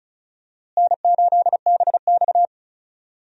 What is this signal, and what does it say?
Morse code, heard as a keyed tone, about 36 wpm, 700 Hz. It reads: N8BX
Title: USA Call Signs - 4 Characters - Repeated 3x - 35wpm